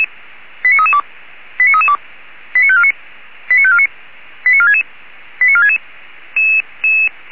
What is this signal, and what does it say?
Five-tone selective call (ZVEI), heard as quick runs of five tones, tones of about 70 ms. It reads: 89332 89332 89549 89549 89466 89466